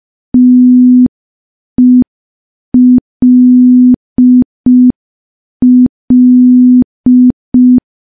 Morse code, heard as a keyed tone, about 5 wpm, 250 Hz. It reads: TELL
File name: morse / morse-training-05WPM